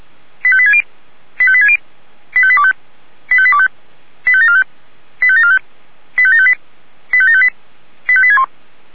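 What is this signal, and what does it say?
Five-tone selective call (ZVEI), heard as quick runs of five tones, tones of about 70 ms. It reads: 85860 85860 85825 85825 85745 85745 85758 85758 85861
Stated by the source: ZVEI five-tone selective call 85827 received via SDR.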